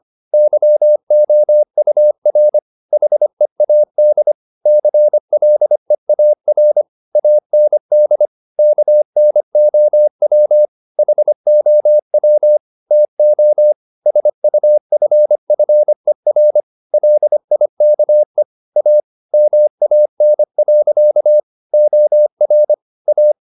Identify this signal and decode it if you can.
Morse code, heard as a keyed tone, about 25 wpm, 600 Hz. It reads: YOUR HEAD CLEAR AND KNOW HOW TO SUFFER LIKE A MAN. OR A